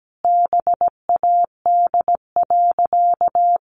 Morse code, heard as a keyed tone, about 17 wpm, 700 Hz. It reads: BAD.
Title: Sentences from Top 1100 Words 17wpm